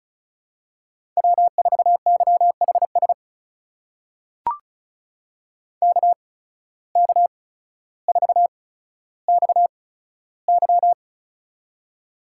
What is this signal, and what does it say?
Morse code, heard as a keyed tone, about 36 wpm, spaced as for 15 wpm, 700 Hz. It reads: W4YHS KK4XY
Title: USA Call Signs - 8x Character Spacing - 35wpm